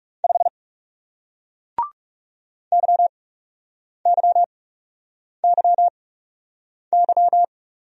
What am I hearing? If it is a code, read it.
Morse code, heard as a keyed tone, about 39 wpm, spaced as for 8 wpm, 700 Hz. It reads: 5 YYYY